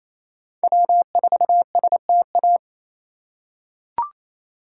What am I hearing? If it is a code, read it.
Morse code, heard as a keyed tone, about 28 wpm, 700 Hz. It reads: W4STA
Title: USA Call Signs - 5 Characters - 2x Character Spacing - 28wpm